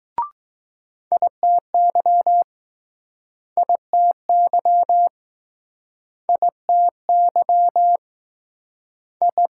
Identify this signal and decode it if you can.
Morse code, heard as a keyed tone, about 21 wpm, 700 Hz. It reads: ITY ITY ITY I